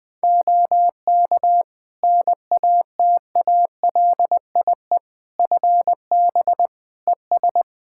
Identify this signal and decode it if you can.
Morse code, heard as a keyed tone, about 20 wpm, 700 Hz. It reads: OK NATALIE FB ES
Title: Practice QSO - Exchange 2 - Info - 20wpm